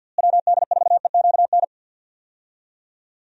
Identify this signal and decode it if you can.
Morse code, heard as a keyed tone, about 48 wpm, 700 Hz. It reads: WD4EQN
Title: USA Call Signs - 6 Characters - Repeated 3x - 50wpm